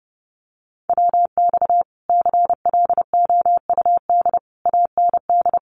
Morse code, heard as a keyed tone, about 30 wpm, 700 Hz. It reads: WX CLOUD AND
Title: Common QSO Phrases Encore 30wpm